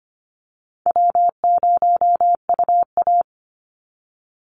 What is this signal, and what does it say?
Morse code, heard as a keyed tone, about 25 wpm, 700 Hz. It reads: W0UA
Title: Easy Call Signs - 25wpm